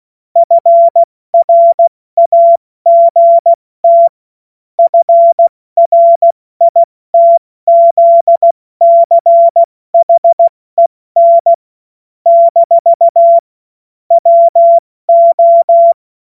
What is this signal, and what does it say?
Morse code, heard as a keyed tone, about 16 wpm, 680 Hz. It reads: FRAGT FRITZCHEN - WO
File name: Morsekurs / Witze / Witz 12. Oktober 2021, 70 F80